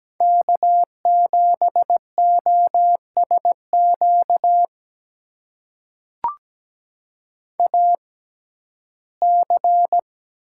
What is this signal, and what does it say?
Morse code, heard as a keyed tone, about 17 wpm, spaced as for 12 wpm, 700 Hz. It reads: K7OSQ AC
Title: USA Call Signs - 5 Characters - 6x Character Spacing - 17wpm